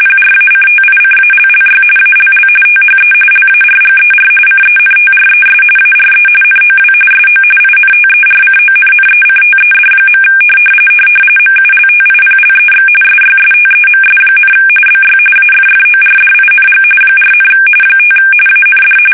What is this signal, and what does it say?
RTTY, about 45.45 baud. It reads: PRACTICALLY ALL THE LARGER CENTRES ARE NOW SO INTERCONNECTED. MULTIPLEX SYSTEMS, BY AFFORDING 4, 5 AND